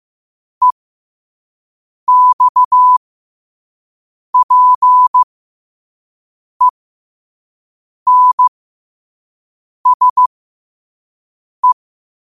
Morse code, heard as a keyed tone, about 15 wpm, spaced as for 5 wpm, 1000 Hz. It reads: EXPENSE
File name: qotdfiles / 5wpm